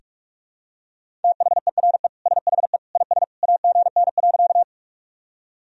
Morse code, heard as a keyed tone, about 47 wpm, 700 Hz. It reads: THERE SHE IS AGN.
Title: Sentences from Top 200 Words and Abbreviations 45wpm